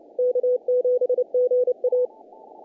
Morse code, heard as a keyed tone, about 29 wpm, 500 Hz. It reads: K7GA
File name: contests / naqp-cw-k5tr-op-k5pi- -1800